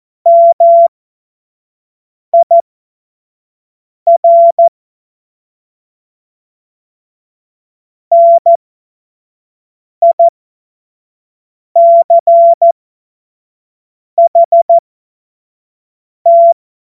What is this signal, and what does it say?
Morse code, heard as a keyed tone, about 14 wpm, spaced as for 5 wpm, 680 Hz. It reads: MIR NICHT
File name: Morsekurs / Witze / Witz 12 März 2024 25 F70